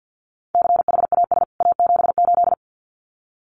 Morse code, heard as a keyed tone, about 50 wpm, 700 Hz. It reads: CHAS A78
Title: CWT Exchange 50wpm